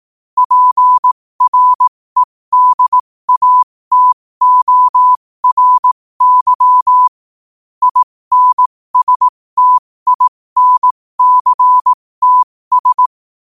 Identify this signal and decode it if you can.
Morse code, heard as a keyed tone, about 18 wpm, spaced as for 15 wpm, 1000 Hz. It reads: PREDATORY INSTINCTS
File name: qotdfiles / 15wpm